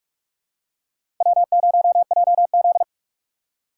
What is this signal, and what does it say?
Morse code, heard as a keyed tone, about 43 wpm, 700 Hz. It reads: W0JZ
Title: USA Call Signs - 4 Characters - 4x Character Spacing - 45wpm